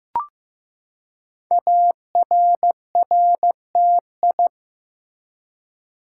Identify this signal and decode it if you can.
Morse code, heard as a keyed tone, about 15 wpm, 700 Hz. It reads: ARRTI
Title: Code Groups - 5 Letters - 15wpm